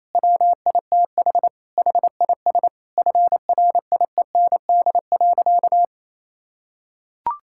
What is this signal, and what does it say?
Morse code, heard as a keyed tone, about 28 wpm, 700 Hz. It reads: WITH HIS FRIEND.
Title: Sentences from Top 1600 Words 28wpm